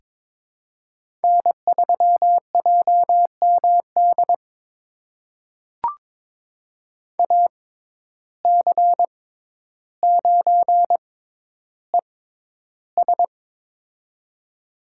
Morse code, heard as a keyed tone, about 22 wpm, spaced as for 11 wpm, 700 Hz. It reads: N3JMD AC9ES
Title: USA Call Signs - 5 Characters - 6x Character Spacing - 22wpm